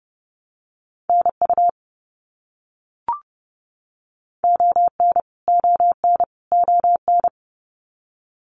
Morse code, heard as a keyed tone, about 30 wpm, 700 Hz. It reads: NU ON ON ON
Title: Canadian Provinces and Territories - Repeated 3x - 30wpm